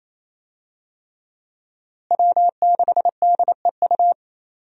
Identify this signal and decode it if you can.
Morse code, heard as a keyed tone, about 28 wpm, 700 Hz. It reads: W6DEU
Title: USA Call Signs - 5 Characters - Repeated 3x - 28wpm